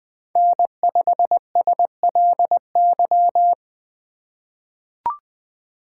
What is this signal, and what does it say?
Morse code, heard as a keyed tone, about 21 wpm, 700 Hz. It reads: N5SLY
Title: 2023 POTA Hunter Call Signs - 20wpm